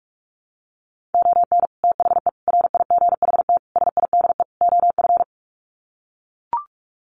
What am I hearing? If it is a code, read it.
Morse code, heard as a keyed tone, about 44 wpm, 700 Hz. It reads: ON THE RIGHT SIDE OF
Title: Sets of 5 Words Encore 45wpm